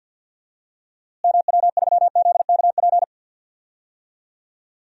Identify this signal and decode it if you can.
Morse code, heard as a keyed tone, about 50 wpm, 700 Hz. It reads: MW3ZKP
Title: DXCC-100 Call Signs - 6 Characters - Repeated 2x - 50wpm